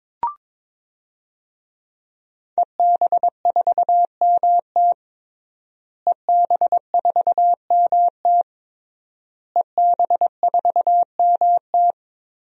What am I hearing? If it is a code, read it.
Morse code, heard as a keyed tone, about 22 wpm, 700 Hz. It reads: EB4MT EB4MT EB4MT
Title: DXCC-100 Call Signs - 5 Characters - Repeated 3x - 22wpm